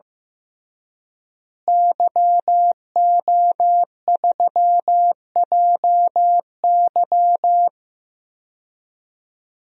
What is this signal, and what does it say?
Morse code, heard as a keyed tone, about 15 wpm, 700 Hz. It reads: YO3JY
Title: DXCC-100 Call Signs - 5 Characters - Repeated 3x - 15wpm